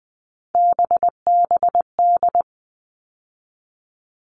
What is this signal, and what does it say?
Morse code, heard as a keyed tone, about 20 wpm, 700 Hz. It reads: BBD
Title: All Cut Numbers - 3-Digit 20wpm